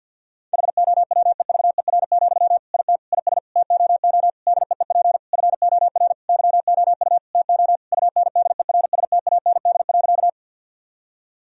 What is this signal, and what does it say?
Morse code, heard as a keyed tone, about 50 wpm, 700 Hz. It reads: HOWEVER, IT IS TOO DEEP FOR YOU TO UNDERSTAND.
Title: Sentences from Top 1000 Words - Review Part 3 - 50wpm